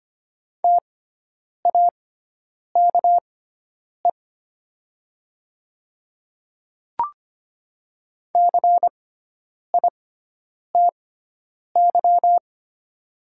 Morse code, heard as a keyed tone, about 25 wpm, spaced as for 9 wpm, 700 Hz. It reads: TAKE CITY
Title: 6x Character Spacing - Top 300 Words - 25wpm